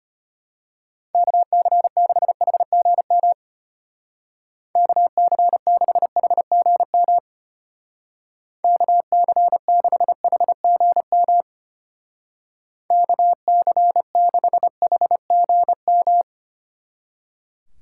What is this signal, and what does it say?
Morse code, heard as a keyed tone, about 32 wpm, 700 Hz. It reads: KC6HGM KC6HGM KC6HGM KC6HGM